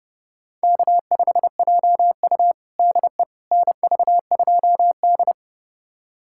Morse code, heard as a keyed tone, about 30 wpm, 700 Hz. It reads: K5JU DE NV2D
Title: USA Callsign DE Callsign - 4 Characters - 30wpm